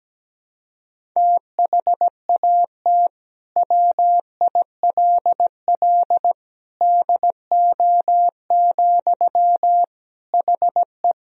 Morse code, heard as a keyed tone, about 17 wpm, 700 Hz. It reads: THAT WILL DO, HE
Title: Sentences from Top 100 Words and Abbreviations 17wpm